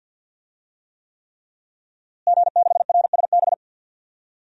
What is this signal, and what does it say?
Morse code, heard as a keyed tone, about 52 wpm, 700 Hz. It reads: G6RSB
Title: DXCC-100 Call Signs - 5 Characters - Repeated 3x - 50wpm